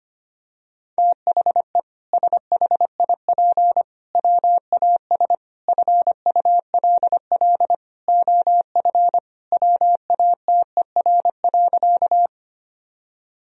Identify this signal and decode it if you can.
Morse code, heard as a keyed tone, about 25 wpm, 700 Hz. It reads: THE SHIP WAS FULL OF WATER.
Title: Sentences from Top 1000 Words - Review Part 3 - 25wpm